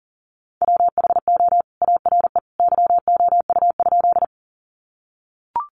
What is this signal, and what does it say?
Morse code, heard as a keyed tone, about 40 wpm, 700 Hz. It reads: WHO ARE YOU?
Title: Sentences from Top 100 Words 40wpm